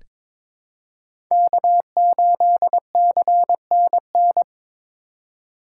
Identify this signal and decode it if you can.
Morse code, heard as a keyed tone, about 22 wpm, 700 Hz. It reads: K8CNN
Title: USA Call Signs - 5 Characters - 2x Character Spacing - 22wpm